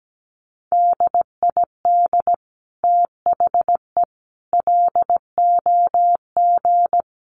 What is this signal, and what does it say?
Morse code, heard as a keyed tone, about 17 wpm, 700 Hz. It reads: DID THE LOG
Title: Sentences from Top 1400 Words 17wpm